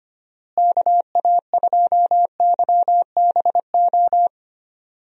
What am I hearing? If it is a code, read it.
Morse code, heard as a keyed tone, about 25 wpm, 700 Hz. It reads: KA2YBO
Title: USA Call Signs - 6 Characters - Repeated 2x - 25wpm